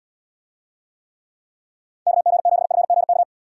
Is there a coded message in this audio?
Yes: Morse code from 2.1 to 3.2 s, heard as a keyed tone, about 78 wpm, 700 Hz. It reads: KG7FGK